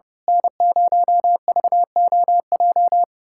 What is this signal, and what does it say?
Morse code, heard as a keyed tone, about 32 wpm, 700 Hz. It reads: N0VOJ
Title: USA Callsign DE Callsign - 5 Characters - 30wpm